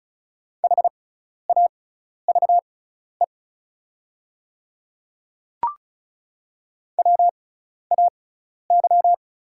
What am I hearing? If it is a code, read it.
Morse code, heard as a keyed tone, about 36 wpm, spaced as for 12 wpm, 700 Hz. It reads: HAVE WAY
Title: 6x Character Spacing - Top 100 Words - 35wpm